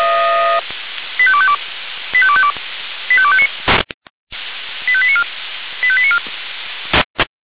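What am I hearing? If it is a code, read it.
Five-tone selective call (ZVEI), heard as quick runs of five tones, tones of about 70 ms. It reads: 85252 85252 85259 85894 85894
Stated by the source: ZVEI five-tone call received on an SDR.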